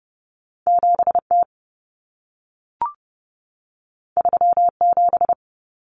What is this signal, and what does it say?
Morse code, heard as a keyed tone, about 30 wpm, 700 Hz. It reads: 7T 37
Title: Common Cut Numbers - 2-Digit 30wpm